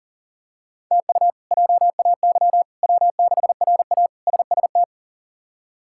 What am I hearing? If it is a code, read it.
Morse code, heard as a keyed tone, about 40 wpm, 700 Hz. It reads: TU JAY W6RA SST
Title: K1USN SST QSO Practice - 40wpm